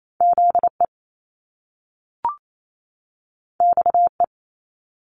Morse code, heard as a keyed tone, about 28 wpm, 700 Hz. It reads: Z E X E
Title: Top Two Letter Combinations in Words 28wpm